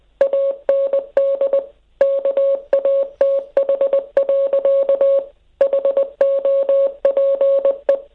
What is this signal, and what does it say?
Morse code, heard as a keyed tone, about 20 wpm, 550 Hz. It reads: AND KATH. HOPE